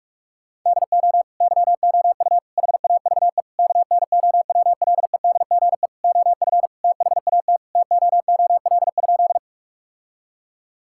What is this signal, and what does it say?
Morse code, heard as a keyed tone, about 41 wpm, 700 Hz. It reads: DO YOU HAVE KNOWLEDGE OF THAT TOOL?
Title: Sentences from Top 1000 Words - Review Part 6 - 45wpm